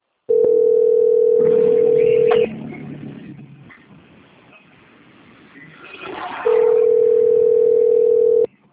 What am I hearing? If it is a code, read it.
Ringback tone (North America), 440 + 480 Hz.